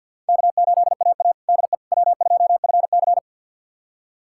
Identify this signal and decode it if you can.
Morse code, heard as a keyed tone, about 52 wpm, 700 Hz. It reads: K8AA DE W2FC
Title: USA Callsign DE Callsign - 4 Characters - 50wpm